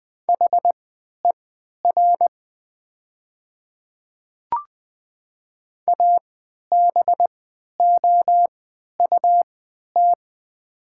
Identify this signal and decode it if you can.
Morse code, heard as a keyed tone, about 20 wpm, spaced as for 11 wpm, 700 Hz. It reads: HER ABOUT
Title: 3x Character Spacing - Top 100 Words - 20wpm